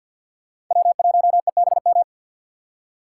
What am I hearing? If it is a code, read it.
Morse code, heard as a keyed tone, about 53 wpm, 700 Hz. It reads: W1EBM
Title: K1USN SST QSO Practice - 50wpm